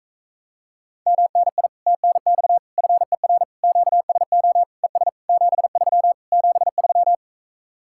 Morse code, heard as a keyed tone, about 45 wpm, 700 Hz. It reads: MNI TNX FER QSO ES 73 73